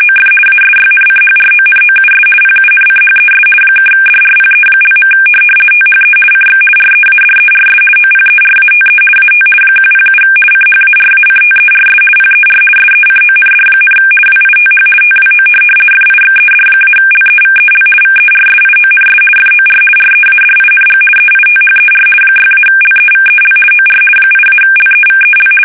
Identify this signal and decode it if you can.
RTTY, about 50 baud. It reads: OF ASYNCHRONOUS PUNCHES (DRPE). THE CX READERS OPERATE AT 107 CHARACTERS PER SECOND, WITH TWO FIXED-LEVEL VERSIONS FOR READING EITHER 6-LEVEL OR 8-LEVEL
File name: fsk850-50B / teletype-corp